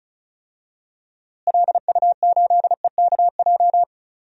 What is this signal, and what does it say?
Morse code, heard as a keyed tone, about 36 wpm, 700 Hz. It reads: LU8EKJ
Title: DXCC-100 Call Signs - 6 Characters - Repeated 3x - 35wpm